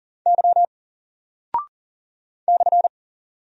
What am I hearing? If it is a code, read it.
Morse code, heard as a keyed tone, about 41 wpm, 700 Hz. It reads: Y /
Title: Single Character (Letter:Number:Punctuation:Prosign) Alt 40wpm